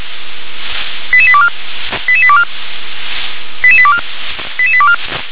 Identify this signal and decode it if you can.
Five-tone selective call (ZVEI), heard as quick runs of five tones, tones of about 70 ms. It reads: 88824 88824 88824 88824